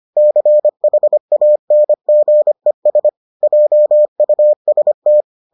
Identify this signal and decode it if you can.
Morse code, heard as a keyed tone, about 25 wpm, 600 Hz. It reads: CHANGES JUST